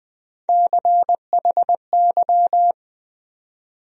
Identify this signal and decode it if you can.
Morse code, heard as a keyed tone, about 20 wpm, 700 Hz. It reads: CHY
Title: USA Call Sign Postfixes - 3 Letters 20wpm